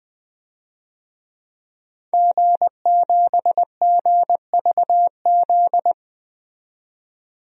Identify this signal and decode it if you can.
Morse code, heard as a keyed tone, about 20 wpm, 700 Hz. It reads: G7GVZ
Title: DXCC-100 Call Signs - 5 Characters - 20wpm